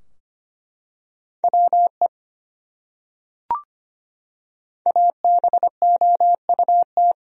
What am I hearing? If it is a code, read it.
Morse code, heard as a keyed tone, about 25 wpm, 700 Hz. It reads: WE ABOUT
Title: Top 100 Words Repeated 25wpm